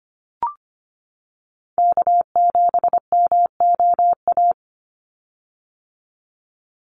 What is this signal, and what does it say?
Morse code, heard as a keyed tone, about 25 wpm, 700 Hz. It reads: K7MOA
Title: Call Signs - 25wpm